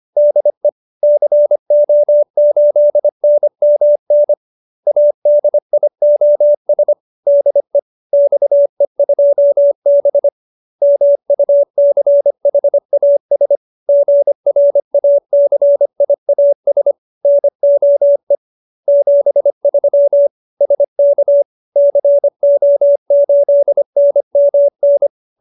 Morse code, heard as a keyed tone, about 25 wpm, 570 Hz. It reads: DE CO8NMN ADIOS DE XE2B MUCHAS GRACIAS NOE 73 SK CO8NMN